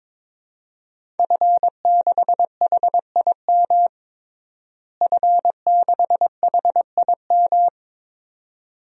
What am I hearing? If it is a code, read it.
Morse code, heard as a keyed tone, about 22 wpm, 700 Hz. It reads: F6HIM F6HIM
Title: DXCC-100 Call Signs - 5 Characters - Repeated 2x - 22wpm